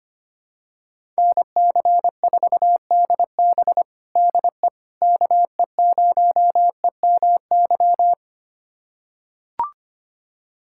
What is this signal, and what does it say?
Morse code, heard as a keyed tone, about 25 wpm, 700 Hz. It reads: NC4DB DE KE0EMY
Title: USA Callsign DE Callsign 25wpm